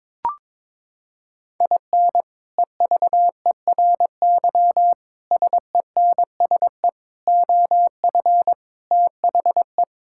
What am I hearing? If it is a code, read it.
Morse code, heard as a keyed tone, about 22 wpm, 700 Hz. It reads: IN EVERY SENSE OF THE
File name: practice / 22wpm / Sets of 6 Words - 22wpm